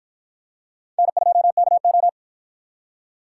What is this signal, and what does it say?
Morse code, heard as a keyed tone, about 51 wpm, 700 Hz. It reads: N2KO